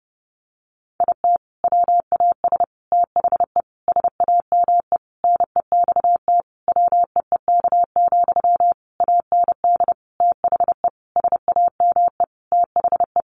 Morse code, heard as a keyed tone, about 30 wpm, 700 Hz. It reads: IT WAS THE SAME NEXT WEEK, AND THE SAME THE
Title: Sentences from Top 500 Words 30wpm